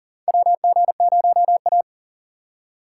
Morse code, heard as a keyed tone, about 39 wpm, 700 Hz. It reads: WG0A
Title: USA Call Signs - 4 Characters - 2x Character Spacing - 40wpm